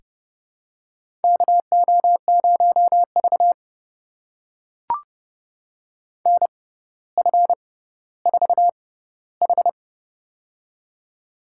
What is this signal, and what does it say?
Morse code, heard as a keyed tone, about 29 wpm, spaced as for 15 wpm, 700 Hz. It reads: KO0V NF4H
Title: USA Call Signs - 4 Characters - 6x Character Spacing - 30wpm